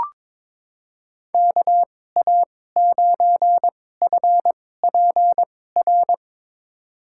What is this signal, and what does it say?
Morse code, heard as a keyed tone, about 22 wpm, spaced as for 16 wpm, 700 Hz. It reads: KA9FPR
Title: USA Call Signs - 6 Characters - 2x Character Spacing - 22wpm